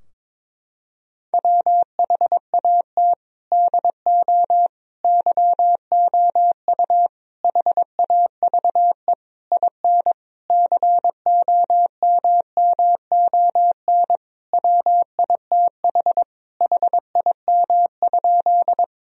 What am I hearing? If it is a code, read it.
Morse code, heard as a keyed tone, about 22 wpm, 700 Hz. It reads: WHAT DO YOU HAVE IN COMMON WITH HIM?
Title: Sentences from Top 1000 Words - Review Part 3 - 22wpm